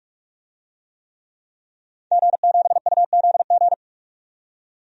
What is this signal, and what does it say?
Morse code, heard as a keyed tone, about 47 wpm, 700 Hz. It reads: G7UZG
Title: DXCC-100 Call Signs - 5 Characters - 45wpm